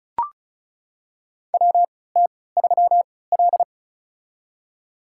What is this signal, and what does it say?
Morse code, heard as a keyed tone, about 34 wpm, spaced as for 20 wpm, 700 Hz. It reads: WT3L
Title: USA Call Signs - 4 Characters - 3x Character Spacing - 35wpm